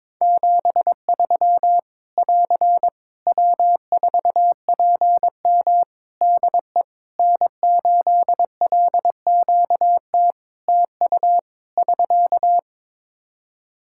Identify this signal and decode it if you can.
Morse code, heard as a keyed tone, about 22 wpm, 700 Hz. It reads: 73 <AR> W4PM DE N8LQT TU <SK>
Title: Practice QSO - Exchange 4 - Ending - 22wpm